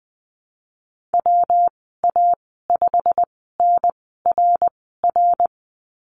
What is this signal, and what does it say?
Morse code, heard as a keyed tone, about 20 wpm, spaced as for 14 wpm, 700 Hz. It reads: WA5NRR
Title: 122 Lesson 20g - Number 3 - Call Signs 20wpm